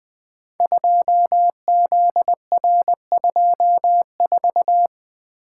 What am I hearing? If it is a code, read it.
Morse code, heard as a keyed tone, about 20 wpm, 700 Hz. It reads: 2ZR24
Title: Code Groups - 5 Letter:Numbers - 20wpm